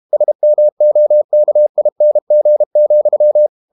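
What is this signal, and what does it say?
Morse code, heard as a keyed tone, about 32 wpm, 600 Hz. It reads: SMOKING,
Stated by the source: Morse code practice at 32 wpm from CW med Gnister, reading Ken Follett's novel Never in English.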